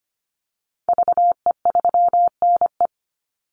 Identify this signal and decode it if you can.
Morse code, heard as a keyed tone, about 25 wpm, 700 Hz. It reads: VE3NE
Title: Call Signs Repeated 2x - 25wpm